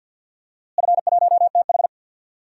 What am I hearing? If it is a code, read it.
Morse code, heard as a keyed tone, about 53 wpm, 700 Hz. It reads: F1TH
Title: DXCC-100 Call Signs - 4 Characters - Repeated 3x - 50wpm